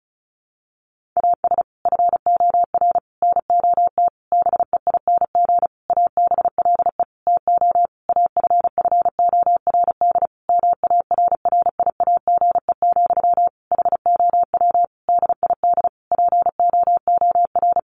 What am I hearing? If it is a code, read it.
Morse code, heard as a keyed tone, about 34 wpm, 700 Hz. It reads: AS FOR NOT BEING ABLE TO AFFORD MARRIAGE, HOW DID POOR
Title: Sentences from Top 1100 Words 35wpm